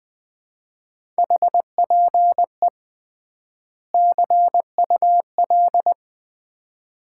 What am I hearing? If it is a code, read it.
Morse code, heard as a keyed tone, about 20 wpm, 700 Hz. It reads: HPE CUL
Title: Common QSO Phrases - 3x Word-Spacing - 20wpm